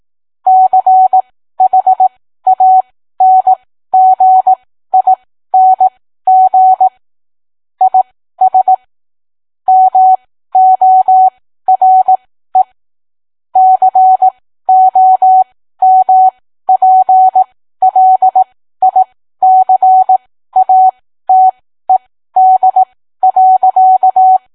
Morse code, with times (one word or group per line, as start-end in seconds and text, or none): 0.5-6.9 CHANGING
7.8-8.7 IS
9.7-12.6 MORE
13.6-24.4 COMPLICATED.